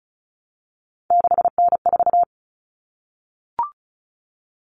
Morse code, heard as a keyed tone, about 34 wpm, 700 Hz. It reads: 6N4
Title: Common Cut Numbers - 3-Digit 35wpm